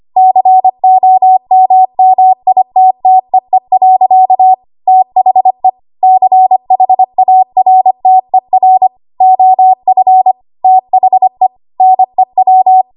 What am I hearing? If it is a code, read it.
Morse code, heard as a keyed tone, about 25 wpm, 750 Hz. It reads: COMMITTEE. THE CHARTER OF THE NEW
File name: Archive / 25 WPM / 25WPM